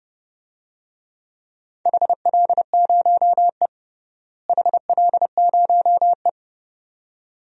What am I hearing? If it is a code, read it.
Morse code, heard as a keyed tone, about 30 wpm, 700 Hz. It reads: HL0E HL0E
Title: DXCC-100 Call Signs - 4 Characters - Repeated 2x - 30wpm